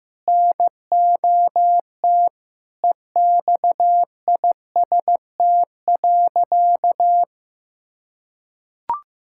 Morse code, heard as a keyed tone, about 15 wpm, 700 Hz. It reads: NOT EXIST.
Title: Sentences from Top 1000 Words - Review Part 5 - 15wpm